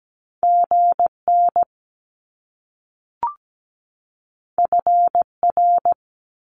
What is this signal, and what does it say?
Morse code, heard as a keyed tone, about 17 wpm, 700 Hz. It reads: GN FR
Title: Top Two Letter Combinations in Words 17wpm